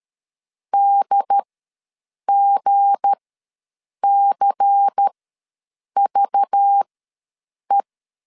Morse code, heard as a keyed tone, about 13 wpm, spaced as for 7 wpm, 800 Hz. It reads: DGCVE
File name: LEZIONI / 8 YJ45 / 8 YJ ws 12wpm 3342